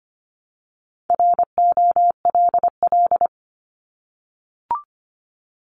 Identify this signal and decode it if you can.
Morse code, heard as a keyed tone, about 25 wpm, 700 Hz. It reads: ROLL
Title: Multi-letter Phonemes 25wpm